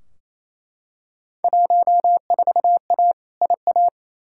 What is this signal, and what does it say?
Morse code, heard as a keyed tone, about 28 wpm, 700 Hz. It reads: 14A IA
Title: Field Day Exchanges Encore - 28wpm